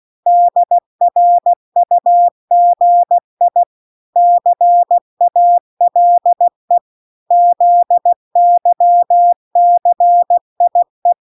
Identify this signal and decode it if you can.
Morse code, heard as a keyed tone, about 16 wpm, 700 Hz. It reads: DRUGI CALE ZYCIE